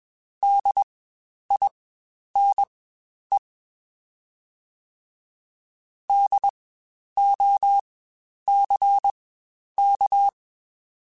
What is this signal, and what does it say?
Morse code, heard as a keyed tone, about 21 wpm, spaced as for 10 wpm, 800 Hz. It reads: DINE DOCK
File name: Spaced Alphas / Spaced Alphas mp3s / Ds sp @20